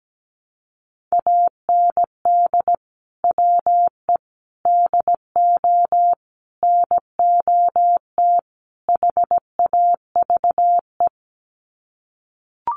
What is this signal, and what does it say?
Morse code, heard as a keyed tone, about 17 wpm, 700 Hz. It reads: AND WE DO NOT HAVE
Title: Sets of 5 Words Encore 17wpm